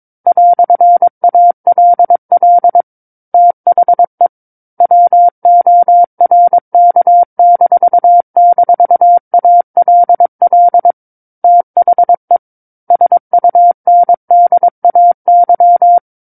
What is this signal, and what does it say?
Morse code, heard as a keyed tone, about 22 wpm, 700 Hz. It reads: "ALL THE WORK--ALL THE SUNDAY
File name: Book One / 13-Chapter 13 - 22 WPM